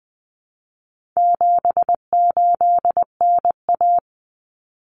Morse code, heard as a keyed tone, about 20 wpm, 700 Hz. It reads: 78NA
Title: Common Cut Numbers - 4-Digit 20wpm